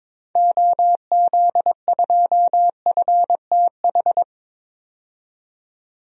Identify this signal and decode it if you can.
Morse code, heard as a keyed tone, about 22 wpm, 700 Hz. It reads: OZ2FTH
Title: DXCC-100 Call Signs - 6 Characters - 22wpm